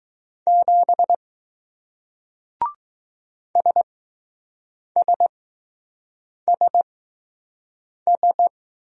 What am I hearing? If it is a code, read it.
Morse code, heard as a keyed tone, about 20 wpm, spaced as for 6 wpm, 700 Hz. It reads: 7 SSSS